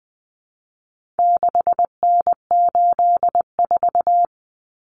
Morse code, heard as a keyed tone, about 20 wpm, 700 Hz. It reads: 6N84
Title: Common Cut Numbers - 4-Digit 20wpm